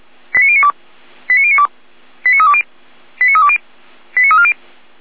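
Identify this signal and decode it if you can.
Five-tone selective call (ZVEI), heard as quick runs of five tones, tones of about 70 ms. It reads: 89092 89092 89320 89320 89340
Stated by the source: ZVEI five-tone call received with an SDR.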